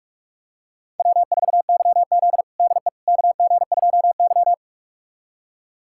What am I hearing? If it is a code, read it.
Morse code, heard as a keyed tone, about 46 wpm, 700 Hz. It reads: W4YZ DE KG2Y
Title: USA Callsign DE Callsign - 4 Characters - 45wpm